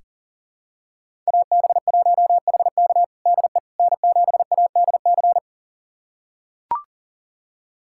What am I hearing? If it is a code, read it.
Morse code, heard as a keyed tone, about 40 wpm, 690 Hz. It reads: AB1HK DE N7ADC
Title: USA Callsign DE Callsign - 5 Characters - 40wpm